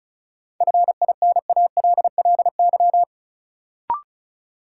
Morse code, heard as a keyed tone, about 36 wpm, 700 Hz. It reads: FINALLY
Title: Sentences from Top 1000 Words - New Word Drill - 35wpm